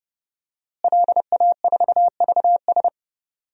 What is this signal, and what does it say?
Morse code, heard as a keyed tone, about 30 wpm, 700 Hz. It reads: LA4VS
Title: DXCC-100 Call Signs - 5 Characters - Repeated 2x - 30wpm